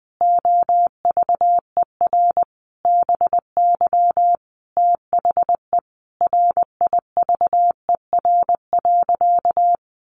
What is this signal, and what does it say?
Morse code, heard as a keyed tone, about 20 wpm, 700 Hz. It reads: OVER BY THE RIVER.
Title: Sentences from Top 1400 Words 20wpm